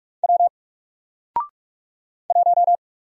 Morse code, heard as a keyed tone, about 44 wpm, 700 Hz. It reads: W 1
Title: Single Letter-Number Alt 45wpm